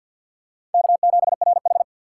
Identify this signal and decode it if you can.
Morse code, heard as a keyed tone, about 50 wpm, 700 Hz. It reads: K7RH
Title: USA Call Signs - 4 Characters - 6x Character Spacing - 50wpm